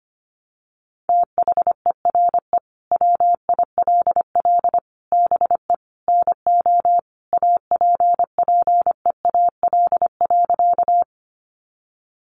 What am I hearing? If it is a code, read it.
Morse code, heard as a keyed tone, about 25 wpm, 700 Hz. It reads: THERE WILL BE NO APPEAL.
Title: Sentences from Top 1300 Words 25wpm